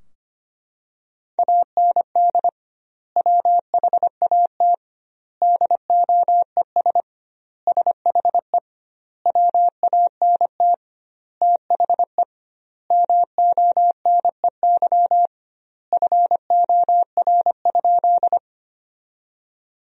Morse code, heard as a keyed tone, about 25 wpm, 700 Hz. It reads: AND WHAT DOES SHE WANT THE MONEY FOR?
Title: 2x Word-Spacing - Sentences from Top 400 Words - 25wpm